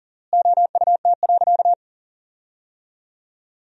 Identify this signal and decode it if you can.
Morse code, heard as a keyed tone, about 40 wpm, 700 Hz. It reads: OUT.
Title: 4x Word-Spacing - Sentences from Top 500 Words - 40wpm